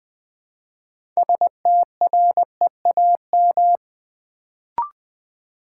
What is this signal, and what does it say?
Morse code, heard as a keyed tone, about 20 wpm, 700 Hz. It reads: STREAM
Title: Build A Word - 3 Letters At A Time - 6-Letter Words - 20wpm